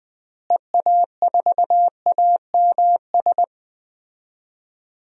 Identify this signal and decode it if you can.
Morse code, heard as a keyed tone, about 20 wpm, 700 Hz. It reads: EA4AMS
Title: DXCC-100 Call Signs - 6 Characters - Repeated 3x - 20wpm